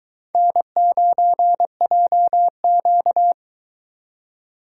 Morse code, heard as a keyed tone, about 22 wpm, 700 Hz. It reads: N9JQ